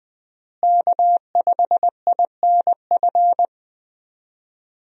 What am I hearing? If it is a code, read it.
Morse code, heard as a keyed tone, about 20 wpm, 700 Hz. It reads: K5INF
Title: USA Call Signs - 5 Characters - Repeated 3x - 20wpm